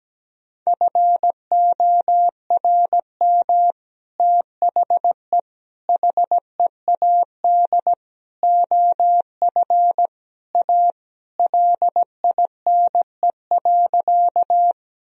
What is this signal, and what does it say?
Morse code, heard as a keyed tone, about 17 wpm, 700 Hz. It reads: FORM THE HEAD OF A LINE.
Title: Sentences from Top 1000 Words - Review Part 1 - 17wpm